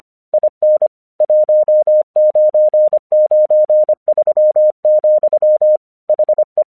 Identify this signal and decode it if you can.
Morse code, heard as a keyed tone, about 26 wpm, 610 Hz. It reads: IN 1993, HE
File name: 1 English 25wpm / AQC13